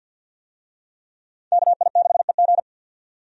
Morse code, heard as a keyed tone, about 47 wpm, 700 Hz. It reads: KI6EG
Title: USA Call Signs - 5 Characters - 50wpm